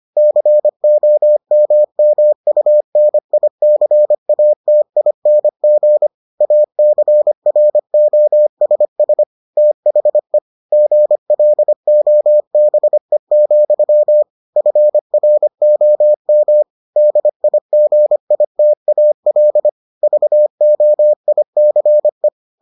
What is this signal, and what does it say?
Morse code, heard as a keyed tone, about 25 wpm, 600 Hz. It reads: COMMUNICATING ACROSS THE GLOBE, FROM DIGITAL VOICE